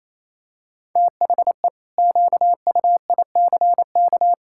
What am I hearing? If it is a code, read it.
Morse code, heard as a keyed tone, about 28 wpm, 700 Hz. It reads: THE QUICK
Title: Pangram Challenge v2 - 28wpm